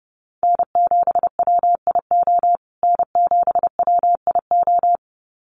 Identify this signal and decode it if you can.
Morse code, heard as a keyed tone, about 30 wpm, 700 Hz. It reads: N7WIO N7WIO
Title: USA Call Signs Repeated 2x 30wpm